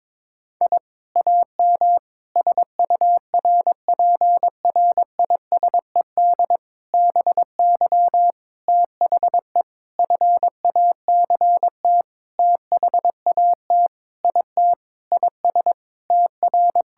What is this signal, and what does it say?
Morse code, heard as a keyed tone, about 22 wpm, 700 Hz. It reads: I AM SURPRISED BY THE FACT THAT IT IS TR
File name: practice / 22wpm / Sentences with 4-Word Phrases - 22wpm